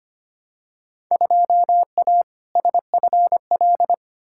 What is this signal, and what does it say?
Morse code, heard as a keyed tone, about 25 wpm, 700 Hz. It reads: 2A SFL
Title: Field Day Exchanges Encore - 25wpm